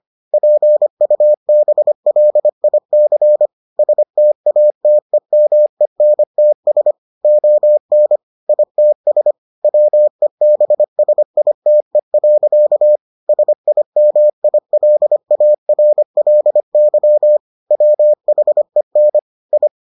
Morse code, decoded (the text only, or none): PUBLIC STATEMENTS ON ITS WEBSITE. SIMILARLY WHEN I